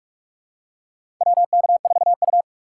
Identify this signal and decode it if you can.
Morse code, heard as a keyed tone, about 46 wpm, 700 Hz. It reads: WK4U
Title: USA Call Signs - 4 Characters - 6x Character Spacing - 45wpm